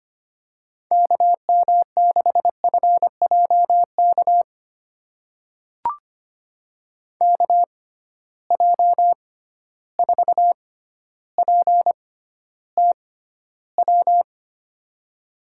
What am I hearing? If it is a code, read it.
Morse code, heard as a keyed tone, about 25 wpm, spaced as for 13 wpm, 700 Hz. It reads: KM6FJK KJ4PTW
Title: USA Call Signs - 6x Character Spacing - 25wpm